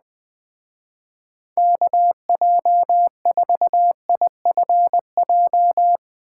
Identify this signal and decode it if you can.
Morse code, heard as a keyed tone, about 20 wpm, 700 Hz. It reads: KJ4IFJ
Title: Speed Racing - USA 2x3 Call Signs 17wpm